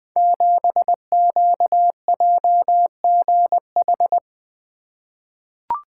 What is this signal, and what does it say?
Morse code, heard as a keyed tone, about 20 wpm, 700 Hz. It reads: 7QJGH
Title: Code Groups - 5 Letter:Numbers - 20wpm